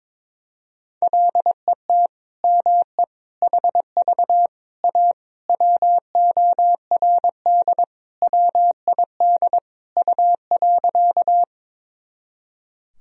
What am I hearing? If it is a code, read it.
Morse code, heard as a keyed tone, about 22 wpm, 700 Hz. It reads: LET ME HV A WORD WID U.
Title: Sentences from Top 300 Words and Abbreviations 22wpm